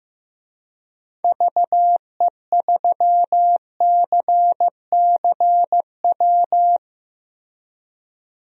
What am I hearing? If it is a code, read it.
Morse code, heard as a keyed tone, about 15 wpm, 700 Hz. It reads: VE3CCW
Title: DXCC-100 Call Signs - 6 Characters - Repeated 3x - 15wpm